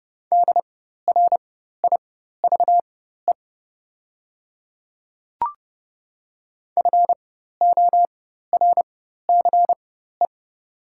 Morse code, heard as a keyed tone, about 30 wpm, spaced as for 14 wpm, 700 Hz. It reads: DRIVE FORCE
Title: 4x Character Spacing - Top 500 Words - 30wpm